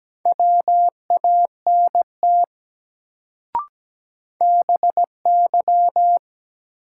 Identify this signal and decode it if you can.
Morse code, heard as a keyed tone, about 17 wpm, 700 Hz. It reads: WANT BY
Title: Sets of 2 Words - New Word Drill - 17wpm